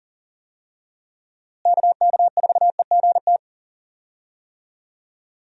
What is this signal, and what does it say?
Morse code, heard as a keyed tone, about 40 wpm, 700 Hz. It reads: KK4EGT
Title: USA Call Signs - 6 Characters - Repeated 3x - 40wpm